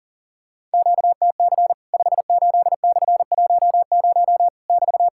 Morse code, heard as a keyed tone, about 40 wpm, 700 Hz. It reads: QTC 58/10 <BT>
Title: Worked All Europe DX Contest - QTC Exchange - 40wpm